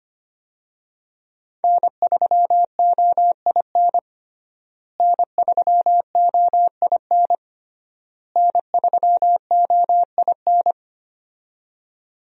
Morse code, heard as a keyed tone, about 25 wpm, 700 Hz. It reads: N3OIN N3OIN N3OIN
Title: USA Call Signs - 5 Characters - Repeated 3x - 25wpm